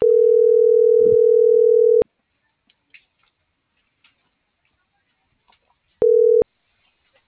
Ringback tone (North America), 440 + 480 Hz.